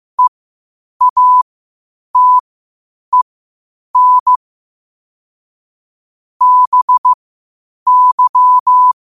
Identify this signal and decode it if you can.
Morse code, heard as a keyed tone, about 15 wpm, spaced as for 8 wpm, 1000 Hz. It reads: EATEN BY
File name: qotdfiles / 8wpm